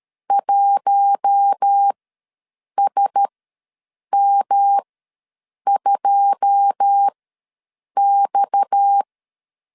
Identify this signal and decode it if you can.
Morse code, heard as a keyed tone, about 13 wpm, spaced as for 7 wpm, 800 Hz. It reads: 1SM2X
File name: LEZIONI / 9 286 ws 12wpm 3351